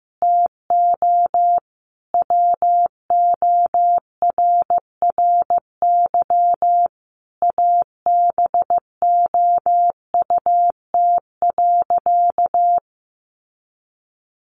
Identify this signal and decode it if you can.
Morse code, heard as a keyed tone, about 15 wpm, 700 Hz. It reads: TO WORRY ABOUT.
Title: Sentences from Top 1100 Words 15wpm